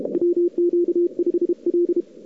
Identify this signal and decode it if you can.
Morse code, heard as a keyed tone, about 32 wpm, 350 Hz. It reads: WQ5L
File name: cw-sprint-k5tr-op-n5rz